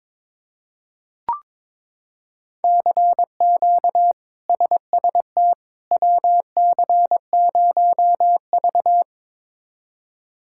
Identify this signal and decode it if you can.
Morse code, heard as a keyed tone, about 22 wpm, 700 Hz. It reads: CQ SST WC0V
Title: K1USN SST QSO Practice - 22wpm